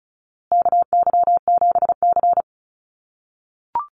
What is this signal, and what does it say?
Morse code, heard as a keyed tone, about 34 wpm, 700 Hz. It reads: KY7C
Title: USA Call Signs Repeated 3x 35wpm